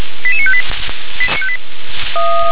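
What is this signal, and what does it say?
Five-tone selective call (ZVEI), heard as quick runs of five tones, tones of about 70 ms. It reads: 88958
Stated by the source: ZVEI five-tone call 88983 received on an SDR.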